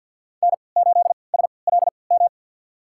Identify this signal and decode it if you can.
Morse code, heard as a keyed tone, about 53 wpm, spaced as for 29 wpm, 700 Hz. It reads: N8SLM
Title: USA Call Signs - 5 Characters - 3x Character Spacing - 50wpm